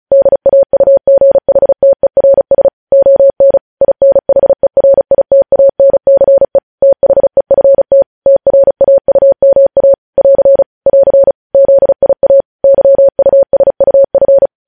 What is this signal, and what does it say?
Morse code, heard as a keyed tone, about 36 wpm, 550 Hz. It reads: DAUGHTERS ON INHERITANCE THEFT TRAUMA <AR> <AR> ZIA YUSUF
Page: news headlines in morse code at 35 wpm